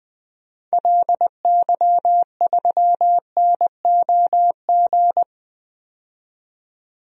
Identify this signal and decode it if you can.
Morse code, heard as a keyed tone, about 20 wpm, 700 Hz. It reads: LY3NOG
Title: DXCC-100 Call Signs - 6 Characters - 20wpm